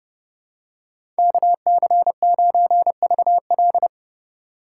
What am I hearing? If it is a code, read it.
Morse code, heard as a keyed tone, about 30 wpm, 700 Hz. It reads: KC9VL
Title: USA Call Signs - 5 Characters - 3x Character Spacing - 30wpm